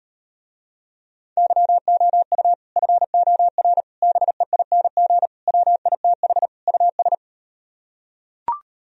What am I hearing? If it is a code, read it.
Morse code, heard as a keyed tone, about 38 wpm, 700 Hz. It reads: YOU FOR BEING WITH US